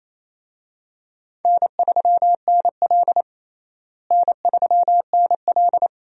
Morse code, heard as a keyed tone, about 28 wpm, 700 Hz. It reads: N3NL N3NL
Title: USA Call Signs - 4 Characters - Repeated 2x - 28wpm